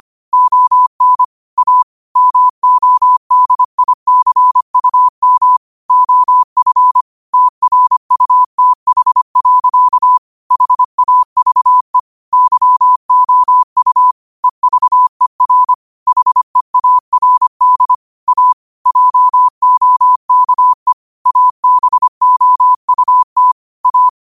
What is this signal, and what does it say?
Morse code, heard as a keyed tone, about 25 wpm, 1000 Hz. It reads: ON A MODICUM OF TRUTH. HAVE YOU EVER HEARD A JOKE ABOUT A